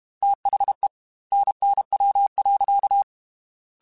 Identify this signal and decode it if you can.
Morse code, heard as a keyed tone, about 32 wpm, 800 Hz. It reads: THE NNW.
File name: Weather Sets / Weather Sets-mp3s / Weather 5 12 @30